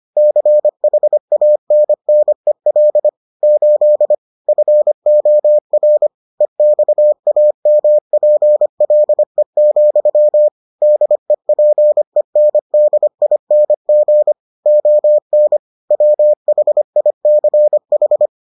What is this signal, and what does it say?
Morse code, heard as a keyed tone, about 25 wpm, 600 Hz. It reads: CHANNEL 8 FOR EXAMPLE, DEPENDING ON WHICH